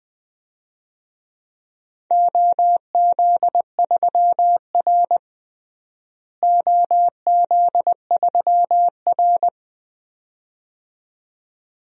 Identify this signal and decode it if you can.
Morse code, heard as a keyed tone, about 21 wpm, 700 Hz. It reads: OZ3R OZ3R
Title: DXCC-100 Call Signs - 4 Characters - Repeated 2x - 20wpm